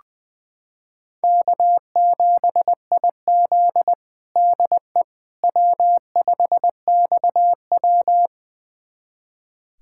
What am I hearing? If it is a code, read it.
Morse code, heard as a keyed tone, about 20 wpm, 700 Hz. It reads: K7IZ DE W5XW
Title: USA Callsign DE Callsign - 4 Characters - 20wpm